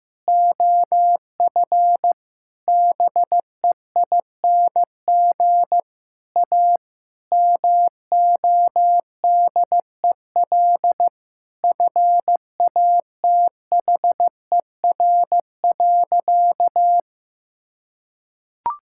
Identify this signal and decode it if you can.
Morse code, heard as a keyed tone, about 16 wpm, 700 Hz. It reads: OF BEING A MODEL FATHER.
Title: Sentences from Top 1000 Words - Review Part 5 - 15wpm